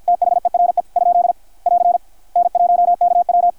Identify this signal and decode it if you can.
Morse code, heard as a keyed tone, about 50 wpm, 660 Hz, with radio noise. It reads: THERE ? <AR> N1KW
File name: archived / CRSnet